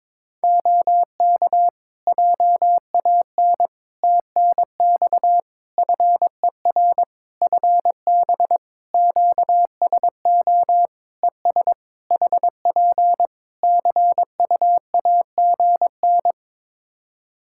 Morse code, heard as a keyed tone, about 22 wpm, 700 Hz. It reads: OK JAN TNX FER FB QSO ES HP CUAGN
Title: K1USN SST QSO Practice 22wpm